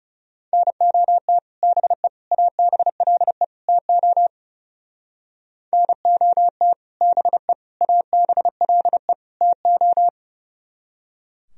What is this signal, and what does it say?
Morse code, heard as a keyed tone, about 39 wpm, 700 Hz. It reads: NOT BE ABLE TO NOT BE ABLE TO